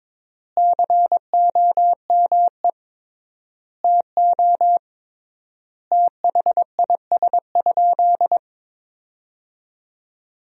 Morse code, heard as a keyed tone, about 21 wpm, 700 Hz. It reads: COME TO THIS?
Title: 3x Word-Spacing - Sentences from Top 100 Words - 22wpm